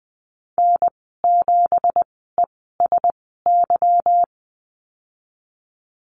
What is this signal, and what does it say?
Morse code, heard as a keyed tone, about 20 wpm, spaced as for 15 wpm, 700 Hz. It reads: N7ESY
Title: 186 Lesson 30g - Letter b - Call Signs 20wpm